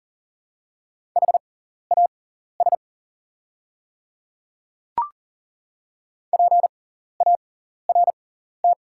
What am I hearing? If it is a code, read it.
Morse code, heard as a keyed tone, about 40 wpm, spaced as for 14 wpm, 690 Hz. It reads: HAS PART
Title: 6x Character Spacing - Top 200 Words - 40wpm